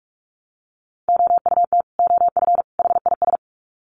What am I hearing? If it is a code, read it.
Morse code, heard as a keyed tone, about 44 wpm, 700 Hz. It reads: OUT OF HIS
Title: Sets of 3 Words 45wpm